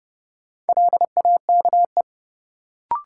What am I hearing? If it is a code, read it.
Morse code, heard as a keyed tone, about 29 wpm, 700 Hz. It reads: LAKE
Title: Sentences from Top 1000 Words - Review Part 3 - New Word Drill - 30wpm